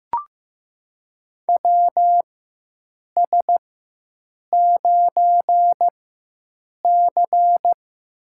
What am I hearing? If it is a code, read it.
Morse code, heard as a keyed tone, about 15 wpm, spaced as for 7 wpm, 700 Hz. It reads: WS9C
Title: USA Call Signs - 4 Characters - 4x Character Spacing - 15wpm